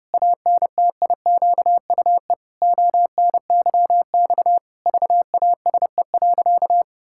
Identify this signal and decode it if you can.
Morse code, heard as a keyed tone, about 32 wpm, 700 Hz. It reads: ANTIQUE ONYX VASE.
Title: Pangram Challenge v2 - 30wpm